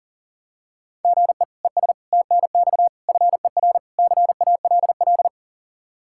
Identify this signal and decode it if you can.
Morse code, heard as a keyed tone, about 40 wpm, 700 Hz. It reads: GE ES TNX FER CALL
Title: K1USN SST QSO Practice 40wpm